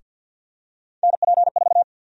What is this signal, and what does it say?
Morse code, heard as a keyed tone, about 51 wpm, 700 Hz. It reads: NP4
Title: USA Call Sign Prefixes - Repeated 2x - 50wpm